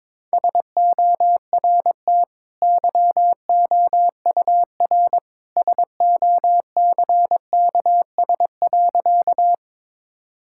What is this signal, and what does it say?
Morse code, heard as a keyed tone, about 22 wpm, 690 Hz. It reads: SORT YOUR SOCKS.
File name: practice / 22wpm / Easy Sentences with Hints - 22wpm